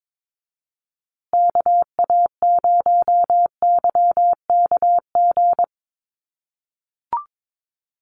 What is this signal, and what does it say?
Morse code, heard as a keyed tone, about 22 wpm, 700 Hz. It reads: KA0YKG
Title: USA Call Signs Repeated 3x 22wpm